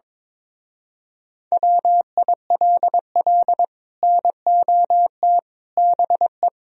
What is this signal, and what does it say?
Morse code, heard as a keyed tone, about 23 wpm, 700 Hz. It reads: WILL NOT BE
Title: Speed-Racing - Sets of 5 Words 22wpm